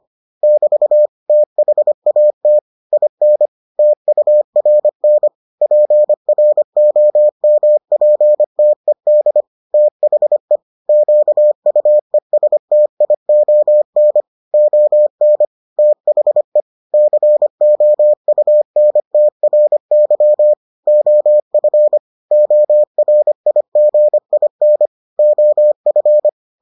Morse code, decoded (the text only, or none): <BT> THAT IN TURN PROMPTED THE QUESTION ON THE COUNTRY OF ORIGIN OF